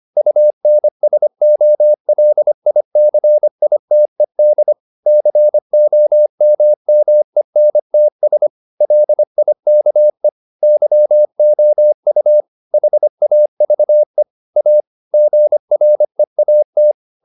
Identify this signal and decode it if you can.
Morse code, heard as a keyed tone, about 25 wpm, 600 Hz. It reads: UNSOLICITED COMMENTS LIKE YOU HAVE A GREAT